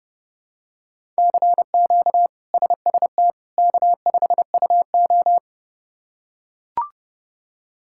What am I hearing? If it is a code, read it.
Morse code, heard as a keyed tone, about 32 wpm, 700 Hz. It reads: CQ SST K5UO
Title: K1USN SST QSO Practice - 30wpm